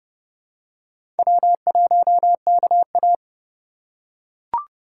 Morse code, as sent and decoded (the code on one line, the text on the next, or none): .-- .---- -.- .-
W1KA